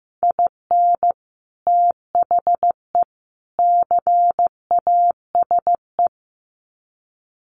Morse code, heard as a keyed tone, about 15 wpm, 700 Hz. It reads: IN THE CASE
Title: Sets of 3 Words 15wpm